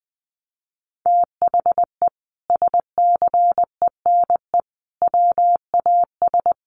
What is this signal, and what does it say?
Morse code, heard as a keyed tone, about 20 wpm, 700 Hz. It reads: THE SCENE WAS
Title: Sentences from Top 1100 Words 20wpm